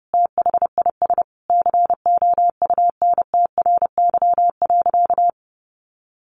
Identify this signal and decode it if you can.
Morse code, heard as a keyed tone, about 30 wpm, 700 Hz. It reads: THIS COUNTRY.
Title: Sentences from Top 1400 Words 30wpm